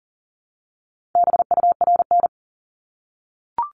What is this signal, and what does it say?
Morse code, heard as a keyed tone, about 40 wpm, 700 Hz. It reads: BURN
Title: Multi-letter Phonemes Repeated 3x 40wpm